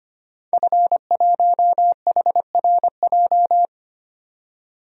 Morse code, heard as a keyed tone, about 25 wpm, 700 Hz. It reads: F1HRJ
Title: DXCC-100 Call Signs - 5 Characters - 25wpm